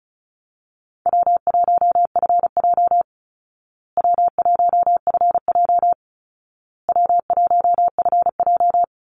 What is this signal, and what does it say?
Morse code, heard as a keyed tone, about 34 wpm, 700 Hz. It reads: W1FJ W1FJ W1FJ
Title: Easy Call Signs Repeated 3x - 35wpm